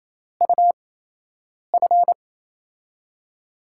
Morse code, heard as a keyed tone, about 28 wpm, spaced as for 8 wpm, 700 Hz. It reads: UF
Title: USA Call Signs - 8x Character Spacing - 28wpm